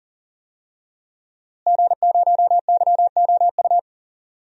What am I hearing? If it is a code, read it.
Morse code, heard as a keyed tone, about 39 wpm, 700 Hz. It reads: G0YOU
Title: DXCC-100 Call Signs - 5 Characters - Repeated 2x - 40wpm